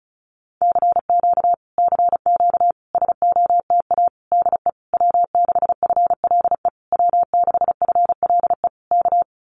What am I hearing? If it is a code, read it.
Morse code, heard as a keyed tone, about 34 wpm, 700 Hz. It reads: CQ CQ SOTA DE W6FLE W6FLE K
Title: Common QSO Phrases Encore 35wpm